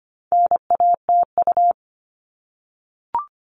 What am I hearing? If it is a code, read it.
Morse code, heard as a keyed tone, about 25 wpm, 700 Hz. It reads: NATU
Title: All Cut Numbers - 4-Digit 25wpm